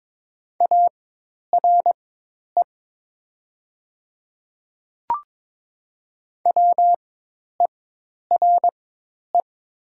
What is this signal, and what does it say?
Morse code, heard as a keyed tone, about 22 wpm, spaced as for 10 wpm, 700 Hz. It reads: ARE WERE
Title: 4x Character Spacing - Top 100 Words - 22wpm